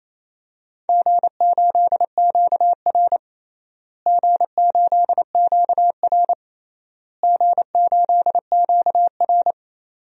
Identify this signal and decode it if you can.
Morse code, heard as a keyed tone, about 28 wpm, 700 Hz. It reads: G8QR G8QR G8QR
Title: DXCC-100 Call Signs - 4 Characters - Repeated 3x - 28wpm